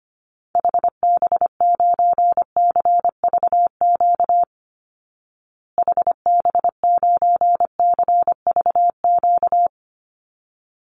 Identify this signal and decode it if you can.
Morse code, heard as a keyed tone, about 25 wpm, 700 Hz. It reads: HB9CVQ HB9CVQ
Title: Call Signs Repeated 3x - 25wpm